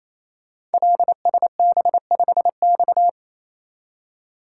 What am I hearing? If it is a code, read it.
Morse code, heard as a keyed tone, about 27 wpm, 700 Hz. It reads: LSB5X
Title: Code Groups - 5 Letter:Numbers - 28wpm